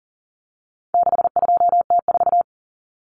Morse code, heard as a keyed tone, about 40 wpm, 700 Hz. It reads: 62T4
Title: Common Cut Numbers - 4-Digit 40wpm